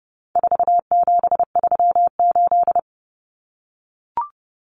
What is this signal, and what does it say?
Morse code, heard as a keyed tone, about 30 wpm, 700 Hz. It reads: 4738
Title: Numbers - 4-Digit 30wpm